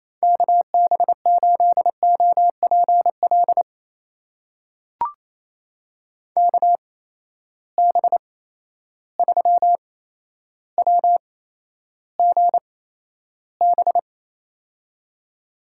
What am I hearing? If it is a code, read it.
Morse code, heard as a keyed tone, about 27 wpm, spaced as for 12 wpm, 700 Hz. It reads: KB8OPL KB3WGB
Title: USA Call Signs - 6 Characters - 8x Character Spacing - 28wpm